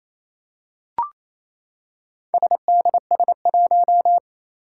Morse code, heard as a keyed tone, about 28 wpm, 700 Hz. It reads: SDS1
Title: Four Challenging Letter-Number Combinations 28wpm